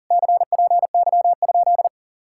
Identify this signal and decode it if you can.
Morse code, heard as a keyed tone, about 49 wpm, 700 Hz. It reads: CPY?
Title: Speed-Racing Common QSO Phrases Encore - 40wpm